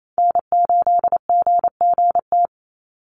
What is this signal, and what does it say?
Morse code, heard as a keyed tone, about 28 wpm, 700 Hz. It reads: N8GGT
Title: USA Call Signs 28wpm